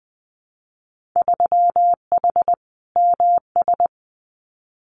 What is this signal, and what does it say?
Morse code, heard as a keyed tone, about 20 wpm, 700 Hz. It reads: 3H MS
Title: Winter Field Day Exchanges Encore 20wpm